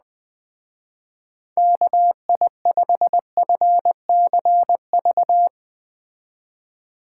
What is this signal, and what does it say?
Morse code, heard as a keyed tone, about 20 wpm, 700 Hz. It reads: KI5FCV